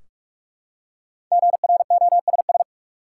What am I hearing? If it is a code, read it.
Morse code, heard as a keyed tone, about 45 wpm, 700 Hz. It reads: GROSS
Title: Five Letter Words - 45wpm